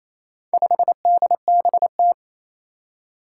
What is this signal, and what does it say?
Morse code, heard as a keyed tone, about 28 wpm, 700 Hz. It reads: 5DBT
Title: Four Challenging Letter-Number Combinations 28wpm